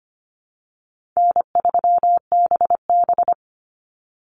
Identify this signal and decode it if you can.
Morse code, heard as a keyed tone, about 25 wpm, 700 Hz. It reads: N3BB
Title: Easy Call Signs Repeated 2x - 25wpm